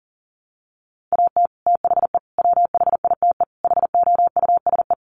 Morse code, heard as a keyed tone, about 40 wpm, 700 Hz. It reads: AT THE WHITE HOUSE
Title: Sets of 4 Words 40wpm